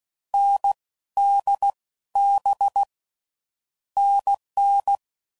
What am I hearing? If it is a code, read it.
Morse code, heard as a keyed tone, about 16 wpm, 800 Hz. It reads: N D B NN
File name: Combinations / Difficult Combos / NDB NDB6